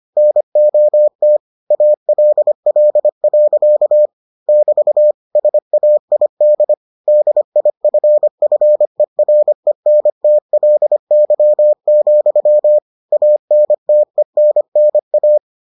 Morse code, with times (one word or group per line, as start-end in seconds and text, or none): 0.2-1.4 NOT
1.7-4.1 ALL.
4.5-5.1 <BT>
5.4-6.7 SAID
7.1-12.8 DIFFERENTLY,
13.1-15.4 ANTENNA